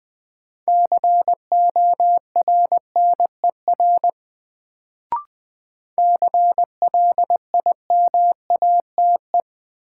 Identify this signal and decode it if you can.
Morse code, heard as a keyed tone, about 21 wpm, 700 Hz. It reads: CORNER CLIMATE
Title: Sentences from Top 1600 Words - New Word Drill - 20wpm